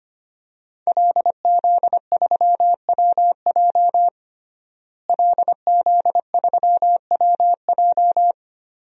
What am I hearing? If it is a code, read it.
Morse code, heard as a keyed tone, about 25 wpm, 700 Hz. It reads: LZ3WJ LZ3WJ
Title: DXCC-100 Call Signs - 5 Characters - Repeated 2x - 25wpm